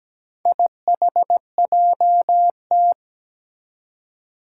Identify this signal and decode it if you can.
Morse code, heard as a keyed tone, about 17 wpm, 700 Hz. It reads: IHJT
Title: Four Challenging Letter-Number Combinations 17wpm